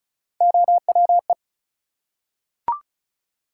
Morse code, heard as a keyed tone, about 33 wpm, 700 Hz. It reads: OWE
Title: Three Letter Words - 35wpm